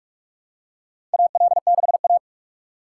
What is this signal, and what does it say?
Morse code, heard as a keyed tone, about 43 wpm, 700 Hz. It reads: AL6A
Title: USA Call Signs - 4 Characters - 4x Character Spacing - 45wpm